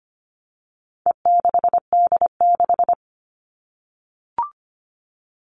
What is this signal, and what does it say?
Morse code, heard as a keyed tone, about 25 wpm, 700 Hz. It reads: E6D6
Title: All Cut Numbers - 4-Digit 25wpm